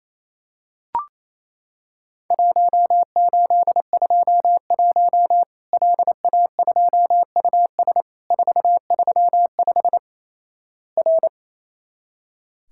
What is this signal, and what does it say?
Morse code, heard as a keyed tone, about 28 wpm, 700 Hz. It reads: 1821 LA2US 435 R
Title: Worked All Europe DX Contest - QTC Exchange - 28wpm